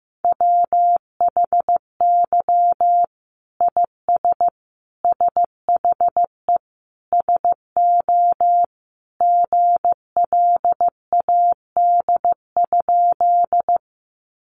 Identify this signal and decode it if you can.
Morse code, heard as a keyed tone, about 15 wpm, 700 Hz. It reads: WHY IS SHE SO GLAD?
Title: Sentences from Top 800 Words 15wpm